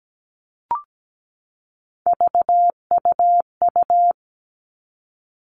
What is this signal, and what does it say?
Morse code, heard as a keyed tone, about 17 wpm, 700 Hz. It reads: VUU
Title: All Cut Numbers - 3-Digit 17wpm